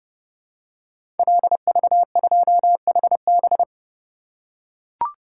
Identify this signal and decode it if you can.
Morse code, heard as a keyed tone, about 28 wpm, 700 Hz. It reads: LV2HB
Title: Code Groups - 5 Letter:Numbers - 30wpm